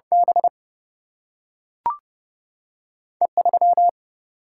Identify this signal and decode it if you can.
Morse code, heard as a keyed tone, about 30 wpm, 700 Hz. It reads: B E 3
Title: Single-Letter Single-Number 30wpm